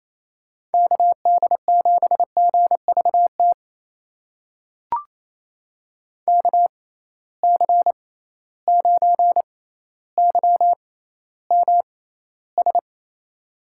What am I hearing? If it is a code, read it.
Morse code, heard as a keyed tone, about 28 wpm, spaced as for 14 wpm, 700 Hz. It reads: KD7GVT KC9YMS
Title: USA Call Signs - 6x Character Spacing - 28wpm